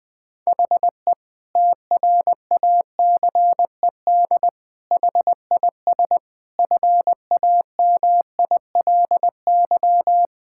Morse code, heard as a keyed tone, about 20 wpm, 700 Hz. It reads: HE TRACED HIS FAMILY
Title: Sentences with 6-Word Phrases - 20wpm